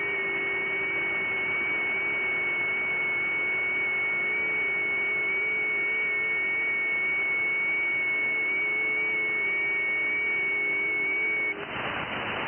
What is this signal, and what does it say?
FT8.
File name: IC-7100